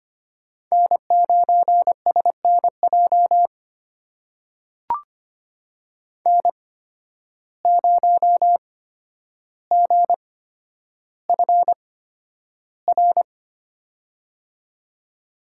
Morse code, heard as a keyed tone, about 25 wpm, spaced as for 11 wpm, 700 Hz. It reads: N9SNJ N0GFR
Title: USA Call Signs - 5 Characters - 8x Character Spacing - 25wpm